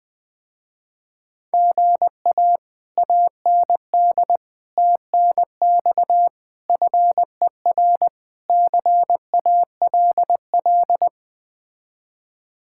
Morse code, heard as a keyed tone, about 20 wpm, 700 Hz. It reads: GA AND TNX FER CALL
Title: Speed-Racing Common QSO Phrases Encore - 17wpm